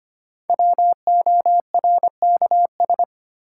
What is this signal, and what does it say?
Morse code, heard as a keyed tone, about 25 wpm, 700 Hz. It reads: WORKS
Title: Sentences from Top 1000 Words - Review Part 5 - New Word Drill - 25wpm